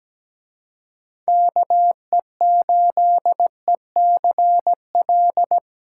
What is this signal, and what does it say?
Morse code, heard as a keyed tone, about 19 wpm, 700 Hz. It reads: KE8ECL
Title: Speed Racing - USA 2x3 Call Signs 17wpm